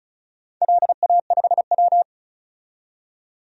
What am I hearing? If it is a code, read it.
Morse code, heard as a keyed tone, about 34 wpm, 700 Hz. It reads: LA5W
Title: DXCC-100 Call Signs - 4 Characters - 35wpm